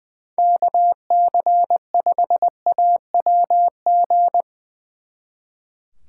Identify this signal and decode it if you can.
Morse code, heard as a keyed tone, about 22 wpm, 700 Hz. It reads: KC5AWG
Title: Speed Racing - USA Call Signs 20wpm